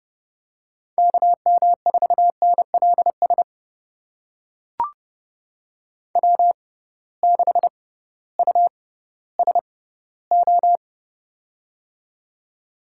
Morse code, heard as a keyed tone, about 32 wpm, spaced as for 17 wpm, 700 Hz. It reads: KM4NLS W6USO
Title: USA Call Signs - 6x Character Spacing - 30wpm